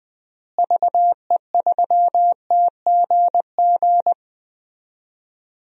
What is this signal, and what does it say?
Morse code, heard as a keyed tone, about 20 wpm, 700 Hz. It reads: VE3TGG
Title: DXCC-100 Call Signs - 6 Characters - Repeated 3x - 20wpm